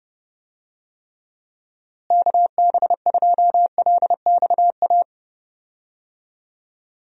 Morse code, heard as a keyed tone, about 30 wpm, 700 Hz. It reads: KB2LXA
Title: USA Call Signs - 6 Characters - Repeated 3x - 30wpm